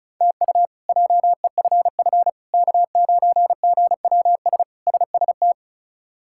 Morse code, heard as a keyed tone, about 36 wpm, 700 Hz. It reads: TU JEFF K9GWS SST
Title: K1USN SST QSO Practice - 35wpm